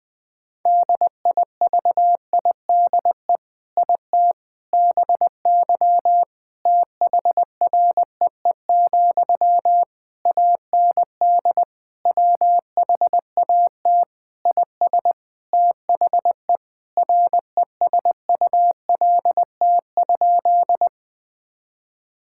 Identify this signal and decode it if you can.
Morse code, heard as a keyed tone, about 20 wpm, 700 Hz. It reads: DIVIDE IT BY THREE, AND WHAT IS THE RESULT?
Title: Sentences from Top 1600 Words 20wpm